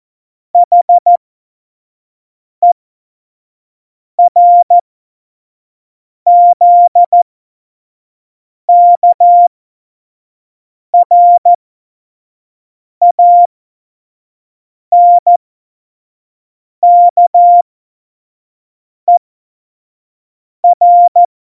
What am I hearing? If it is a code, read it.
Morse code, heard as a keyed tone, about 14 wpm, spaced as for 5 wpm, 700 Hz. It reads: HERZKRANKER
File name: Morsekurs / Witze / Witz 01. Oktober 2019, 25 F70